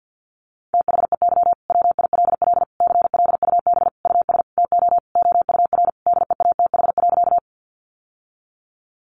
Morse code, heard as a keyed tone, about 50 wpm, 700 Hz. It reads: THEY WILL CLUB US TO OUR DEATH.
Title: Sentences from Top 1200 Words 50wpm